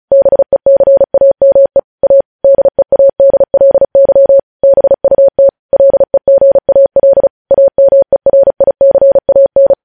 Morse code, heard as a keyed tone, about 36 wpm, 550 Hz. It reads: BECAME A DEADLY BUT LEGAL AMERICAN
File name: podcast / News headlines-1x-35x35